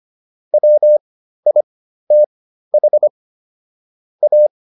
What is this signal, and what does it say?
Morse code, heard as a keyed tone, about 25 wpm, spaced as for 13 wpm, 600 Hz. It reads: WITH A